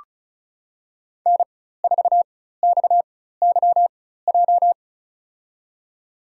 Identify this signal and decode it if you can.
Morse code, heard as a keyed tone, about 34 wpm, spaced as for 16 wpm, 700 Hz. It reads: N4XYJ
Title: USA Call Signs - 5 Characters - 4x Character Spacing - 35wpm